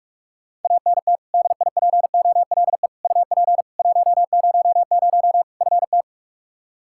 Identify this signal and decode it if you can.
Morse code, heard as a keyed tone, about 45 wpm, 700 Hz. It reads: ANT DIPOLE UP 100 FT
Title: Practice QSO - Exchange 2 - Info - 45wpm